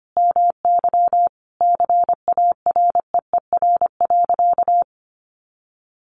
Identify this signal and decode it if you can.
Morse code, heard as a keyed tone, about 25 wpm, 700 Hz. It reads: MY CAREER.
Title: Sentences from Top 1200 Words 25wpm